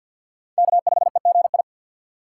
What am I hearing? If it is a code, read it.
Morse code, heard as a keyed tone, about 46 wpm, 700 Hz. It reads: K5EGI
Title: USA Call Signs - 4x Character Spacing - 50wpm